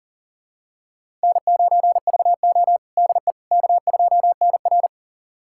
Morse code, heard as a keyed tone, about 40 wpm, 700 Hz. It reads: N9VO DE K2NR
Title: USA Callsign DE Callsign - 4 Characters - 40wpm